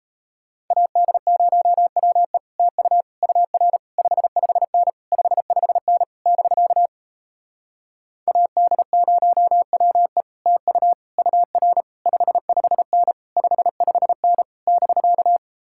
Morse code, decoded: AD0WE TU UR 55N 55N <BK> AD0WE TU UR 55N 55N <BK>

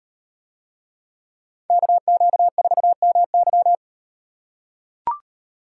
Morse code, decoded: KQ4MY

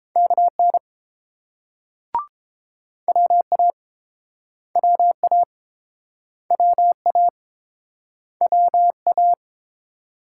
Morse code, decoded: KN WA WA WA WA

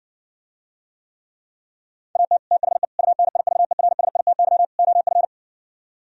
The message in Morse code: AT THE UNIVERSITY OF

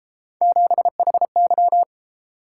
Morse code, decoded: 7HY

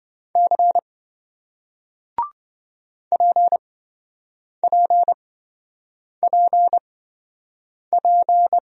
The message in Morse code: C PPPP

